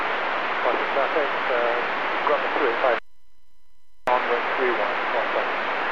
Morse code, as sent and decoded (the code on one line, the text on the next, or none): none